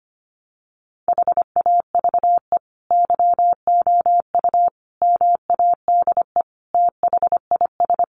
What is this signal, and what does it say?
Morse code, heard as a keyed tone, about 25 wpm, 700 Hz. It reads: HAVE YOU MADE THIS
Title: Sentences from Top 1300 Words 25wpm